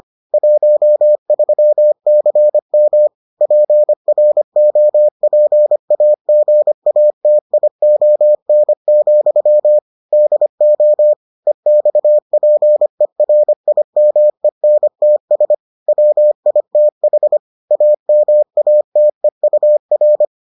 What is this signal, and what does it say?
Morse code, heard as a keyed tone, about 25 wpm, 600 Hz. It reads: 13CM PROPAGATION, DO EXPERIMENTS WITH AMATEUR